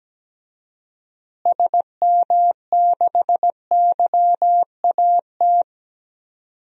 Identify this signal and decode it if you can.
Morse code, heard as a keyed tone, about 17 wpm, 700 Hz. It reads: SM6YAT
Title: DXCC-100 Call Signs - 6 Characters - Repeated 2x - 17wpm